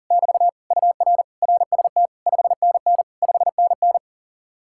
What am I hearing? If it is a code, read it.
Morse code, heard as a keyed tone, about 52 wpm, 700 Hz. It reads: <BT> UR RST 5NN 5NN